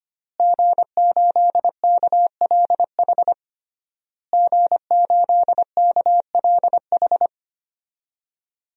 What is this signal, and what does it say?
Morse code, heard as a keyed tone, about 25 wpm, 700 Hz. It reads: G8KLH G8KLH
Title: DXCC-100 Call Signs - 5 Characters - Repeated 3x - 25wpm